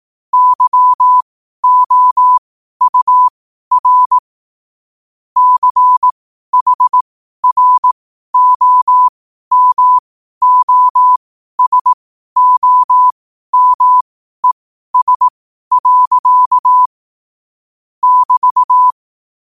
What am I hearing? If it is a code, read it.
Morse code, heard as a keyed tone, about 18 wpm, spaced as for 12 wpm, 1000 Hz. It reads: YOUR CHROMOSOMES. <BT>
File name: qotdfiles / 12wpm